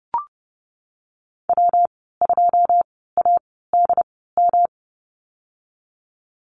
Morse code, heard as a keyed tone, about 30 wpm, 700 Hz. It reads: W 2 A D M
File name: Lesson 19g - Letter w - Call Signs 30wpm